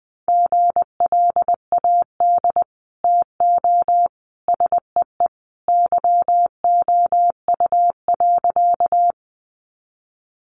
Morse code, heard as a keyed tone, about 20 wpm, 700 Hz. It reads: GLAD TO SEE YOU.
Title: Sentences from Top 800 Words 20wpm